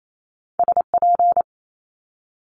Morse code, heard as a keyed tone, about 28 wpm, 700 Hz. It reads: SP
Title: Multi-letter Phonemes Quick Fire 28wpm